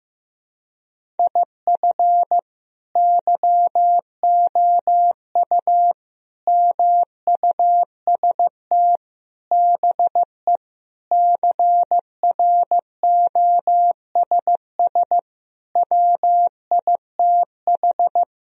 Morse code, heard as a keyed tone, about 15 wpm, 700 Hz. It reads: IF YOU MUST BE CROSS WITH
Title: Sentences from Top 1000 Words - Review Part 1 - 15wpm